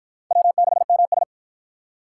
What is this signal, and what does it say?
Morse code, heard as a keyed tone, about 50 wpm, 700 Hz. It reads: W6MS